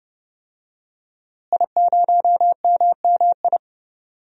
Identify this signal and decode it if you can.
Morse code, heard as a keyed tone, about 29 wpm, 700 Hz. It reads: I0MMI
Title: DXCC-100 Call Signs - 5 Characters - Repeated 3x - 30wpm